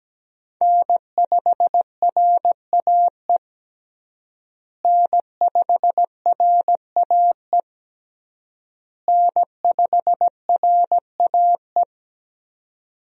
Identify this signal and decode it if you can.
Morse code, heard as a keyed tone, about 17 wpm, 700 Hz. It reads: N5RAE N5RAE N5RAE
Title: USA Call Signs - 5 Characters - Repeated 3x - 17wpm